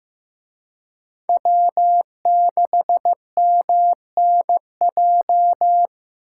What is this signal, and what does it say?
Morse code, heard as a keyed tone, about 18 wpm, 700 Hz. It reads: W6MNJ
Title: Speed Racing - USA 1x3 Call Signs 15wpm